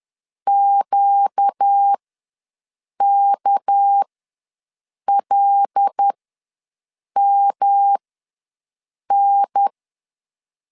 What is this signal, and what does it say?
Morse code, heard as a keyed tone, about 10 wpm, spaced as for 6 wpm, 800 Hz. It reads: QKLMN
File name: LEZIONI / 9 2867 ws 10wpm 4215